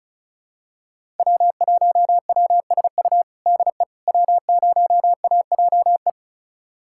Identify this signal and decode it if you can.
Morse code, heard as a keyed tone, about 34 wpm, 700 Hz. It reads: W1WSU DE W0AJE
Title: USA Callsign DE Callsign - 5 Characters - 35wpm